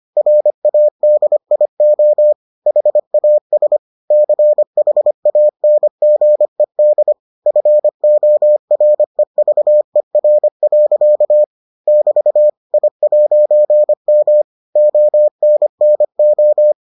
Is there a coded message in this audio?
Yes: Morse code from 0.2 to 16.7 s, heard as a keyed tone, about 25 wpm, 600 Hz. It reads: RADIO HAS CHANGED FOREVER. <BT> I'M ONNO